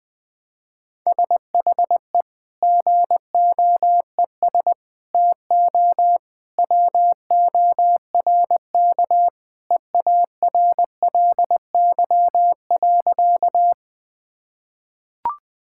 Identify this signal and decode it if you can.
Morse code, heard as a keyed tone, about 20 wpm, 700 Hz. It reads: SHE GOES TO WORK EARLY.
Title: Sentences with 2-Word Phrases - 20wpm